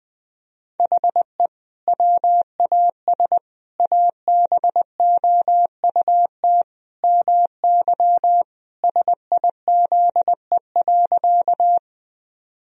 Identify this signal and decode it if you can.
Morse code, heard as a keyed tone, about 20 wpm, 700 Hz. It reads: HE WAS ABOUT MY SIZE.
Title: Sentences from Top 1000 Words - Review Part 2 - 20wpm